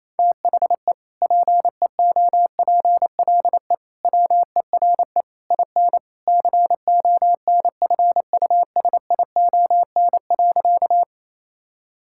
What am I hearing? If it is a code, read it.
Morse code, heard as a keyed tone, about 28 wpm, 700 Hz. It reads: THE PEOPLE WERE IN CONFUSION.
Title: Sentences from Top 1600 Words 28wpm